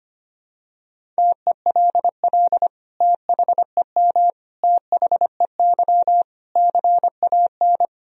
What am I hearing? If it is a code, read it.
Morse code, heard as a keyed tone, about 25 wpm, 700 Hz. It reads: TELL THEM THEY CAN
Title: Sentences from Top 1000 Words - Review Part 2 - 25wpm